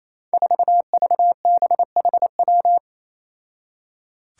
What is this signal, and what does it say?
Morse code, heard as a keyed tone, about 27 wpm, 700 Hz. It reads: 4VBHW
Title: Five Challenging Letter-Number Combinations 28wpm